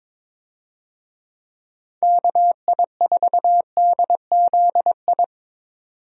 Morse code, heard as a keyed tone, about 22 wpm, 700 Hz. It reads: KI4DZI
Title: USA Call Signs - 6 Characters - 22wpm